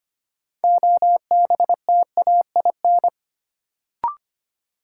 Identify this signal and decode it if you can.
Morse code, heard as a keyed tone, about 25 wpm, 700 Hz. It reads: OBTAIN
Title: Sentences from Top 1300 Words - New Word Drill Alt - 25wpm